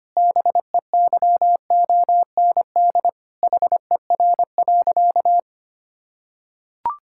Morse code, heard as a keyed tone, about 25 wpm, 700 Hz. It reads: BEYOND HER.
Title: Sentences from Top 1000 Words - Review Part 6 - 25wpm